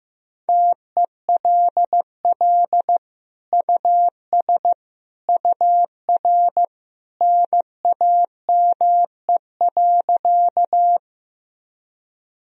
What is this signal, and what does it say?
Morse code, heard as a keyed tone, about 16 wpm, 700 Hz. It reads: TELL US UR NAME.
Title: Sentences from Top 200 Words and Abbreviations 15wpm